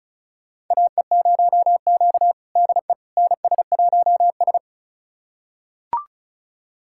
Morse code, heard as a keyed tone, about 34 wpm, 700 Hz. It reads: AE0Q DE NS1S
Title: USA Callsign DE Callsign - 4 Characters - 35wpm